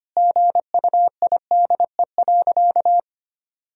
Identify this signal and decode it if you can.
Morse code, heard as a keyed tone, about 25 wpm, 700 Hz. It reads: GUIDE.
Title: Sentences from Top 1000 Words - Review Part 3 - 25wpm